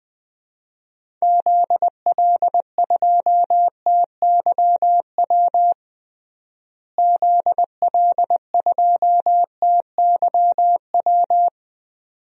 Morse code, heard as a keyed tone, about 19 wpm, 700 Hz. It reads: ZL2TYW ZL2TYW
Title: DXCC-100 Call Signs - 6 Characters - Repeated 2x - 20wpm